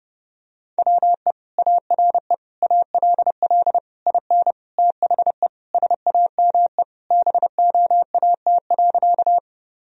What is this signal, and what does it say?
Morse code, heard as a keyed tone, about 32 wpm, 700 Hz. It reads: WE ARE ALL IN THE SAME BOAT.
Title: Sentences from Top 1000 Words - Review Part 3 - 30wpm